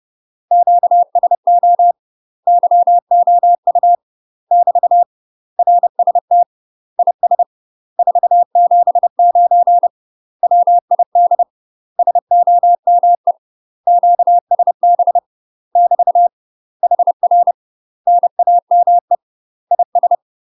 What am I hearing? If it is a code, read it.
Morse code, heard as a keyed tone, about 30 wpm, 700 Hz. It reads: QSO YOU <BT> RST IS 479 WID SOME QSB <BT> HR NAME IS
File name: UK MASTER QSOs TRAINING / 30WPM / UK MASTER QSOs 0035